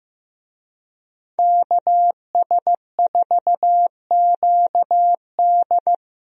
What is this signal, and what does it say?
Morse code, heard as a keyed tone, about 15 wpm, 700 Hz. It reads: KS4QD
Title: USA Call Signs - 5 Characters - Repeated 2x - 15wpm